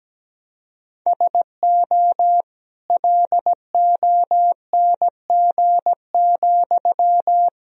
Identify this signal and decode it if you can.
Morse code, heard as a keyed tone, about 17 wpm, 700 Hz. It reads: SO LONG,
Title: Sentences from Top 1000 Words - Review Part 4 - 17wpm